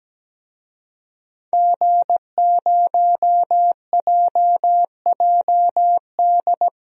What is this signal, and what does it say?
Morse code, heard as a keyed tone, about 17 wpm, 700 Hz. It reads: G0JJD
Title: DXCC-100 Call Signs - 5 Characters - 17wpm